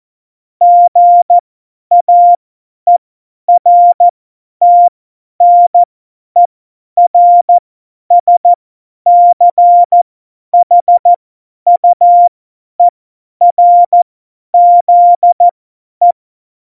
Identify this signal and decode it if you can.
Morse code, heard as a keyed tone, about 14 wpm, spaced as for 10 wpm, 700 Hz. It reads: GAERTNERSCHUERZE
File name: Morsekurs / Witze / Witz 17. November 2020, 50 F70